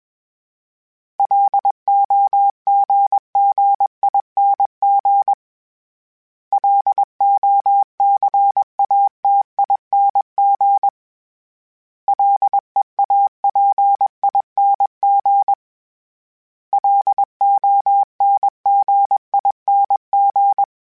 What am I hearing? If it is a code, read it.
Morse code, heard as a keyed tone, about 21 wpm, 800 Hz. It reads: LOGGING LOCATING LEAPING LONGING
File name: Special INGs / Special INGs-mp3s / Sngl Ing @20